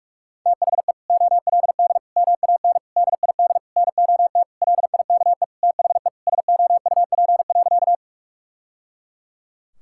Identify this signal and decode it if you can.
Morse code, heard as a keyed tone, about 46 wpm, 700 Hz. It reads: THE OLD MAN DID NOT LIKE THE SOUP.
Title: Sentences from Top 1000 Words - Review Part 6 - 45wpm